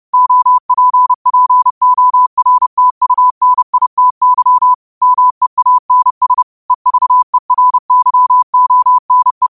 Morse code, heard as a keyed tone, about 30 wpm, 1000 Hz. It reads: OPPORTUNITY MEANS EVERYONE
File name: qotdfiles / 30wpm